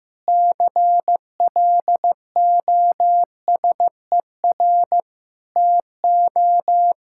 Morse code, heard as a keyed tone, about 15 wpm, 700 Hz. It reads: CLOSER TO
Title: Pangram Challenge v2 - 15wpm